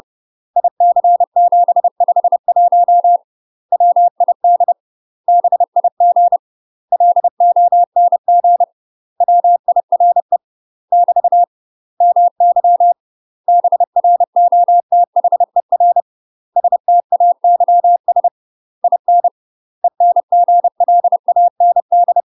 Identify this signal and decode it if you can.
Morse code, heard as a keyed tone, about 30 wpm, 700 Hz. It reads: IC751 WID BIG LONG WIRE <BT> MY BROTHER STAYS IN ENGLAND
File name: UK MASTER QSOs TRAINING / 30WPM / UK MASTER QSOs 0014